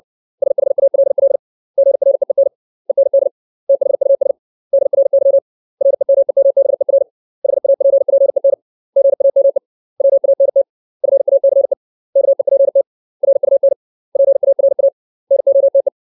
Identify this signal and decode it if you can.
Morse code, heard as a keyed tone, about 60 wpm, 550 Hz. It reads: SHALL GREEN END THUS DRY REMEMBER HAPPEN GAME WANT FACE KEPT RUN PARA NOTE
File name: 632 words / 632 Words 60WPM